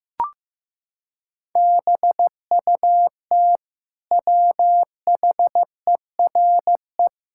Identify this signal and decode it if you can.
Morse code, heard as a keyed tone, about 14 wpm, 700 Hz. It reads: BUT WHERE
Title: Sentences from Top 1600 Words 15wpm